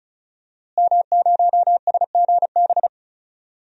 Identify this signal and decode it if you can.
Morse code, heard as a keyed tone, about 34 wpm, 700 Hz. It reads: M0SGB
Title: DXCC-100 Call Signs - 5 Characters - Repeated 3x - 35wpm